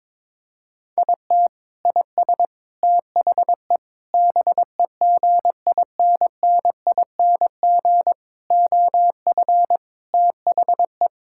Morse code, heard as a keyed tone, about 22 wpm, 700 Hz. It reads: IT IS THE BEGINNING OF THE
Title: Sentences from Top 1000 Words - Review Part 5 - 22wpm